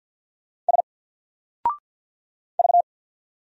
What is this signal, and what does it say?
Morse code, heard as a keyed tone, about 49 wpm, 700 Hz. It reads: S V
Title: Single Character (Letter:Number:Punctuation:Prosign) Alt 50wpm